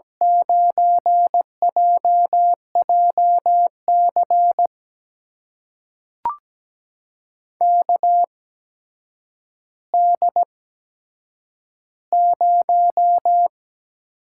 Morse code, heard as a keyed tone, about 17 wpm, spaced as for 8 wpm, 700 Hz. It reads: 9JJC KD0
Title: USA Call Signs - 8x Character Spacing - 17wpm